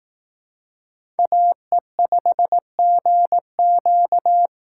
Morse code, heard as a keyed tone, about 18 wpm, 700 Hz. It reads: AE5GQ
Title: Speed Racing - USA 2x2 Call Signs 15wpm